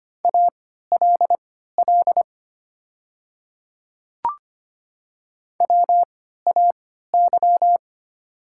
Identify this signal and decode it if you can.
Morse code, heard as a keyed tone, about 25 wpm, spaced as for 14 wpm, 700 Hz. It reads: ALL WAY
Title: 3x Character Spacing - Top 100 Words - 25wpm